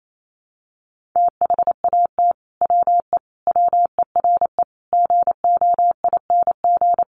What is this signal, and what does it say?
Morse code, heard as a keyed tone, about 28 wpm, 700 Hz. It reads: THAT WE WERE GOING
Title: Sets of 5 Words Encore2 28wpm